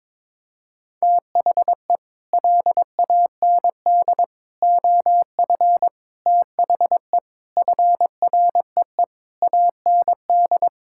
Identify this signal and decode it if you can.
Morse code, heard as a keyed tone, about 22 wpm, 700 Hz. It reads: THE LAND OF THE FREE AND
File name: practice / 22wpm / Sets of 8 Words - 22wpm